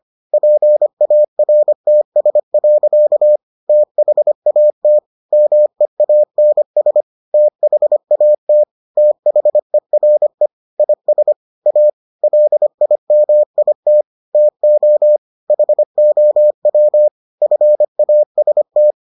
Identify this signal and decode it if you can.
Morse code, heard as a keyed tone, about 25 wpm, 600 Hz. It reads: PARTS. THAT MEANS THAT THERE IS A LIMIT TO HOW FAST